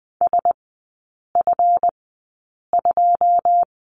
Morse code, heard as a keyed tone, about 20 wpm, spaced as for 8 wpm, 700 Hz. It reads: SF2
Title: 138 Lesson 23e - Letter f - Three Characters 20wpm